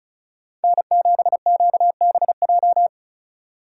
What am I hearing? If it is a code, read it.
Morse code, heard as a keyed tone, about 43 wpm, 700 Hz. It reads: N7QBJ
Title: Speed Racing - USA Call Signs 35wpm